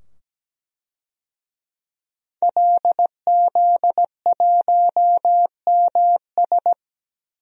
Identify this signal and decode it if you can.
Morse code, heard as a keyed tone, about 17 wpm, 700 Hz. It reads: LZ1MS
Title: DXCC-100 Call Signs - 5 Characters - 17wpm